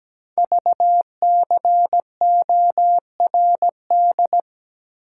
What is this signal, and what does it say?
Morse code, heard as a keyed tone, about 17 wpm, 700 Hz. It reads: VCORD
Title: Code Groups - 5 Letters - 17wpm